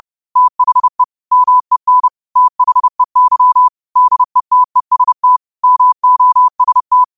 Morse code, heard as a keyed tone, about 30 wpm, 1000 Hz. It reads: THE MEN THEY DETEST MOST